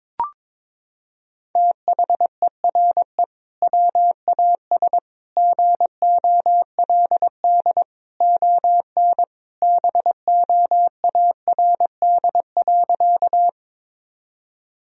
Morse code, heard as a keyed tone, about 22 wpm, 700 Hz. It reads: THERE WAS GOLD ON BOARD.
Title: Sentences from Top 1000 Words - Review Part 4 - 22wpm